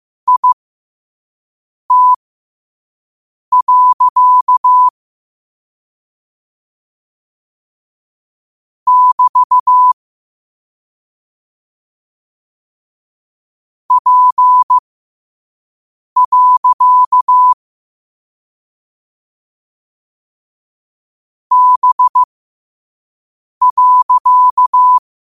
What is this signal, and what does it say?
Morse code, heard as a keyed tone, about 15 wpm, spaced as for 5 wpm, 1000 Hz. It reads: IT. <BT> P. B.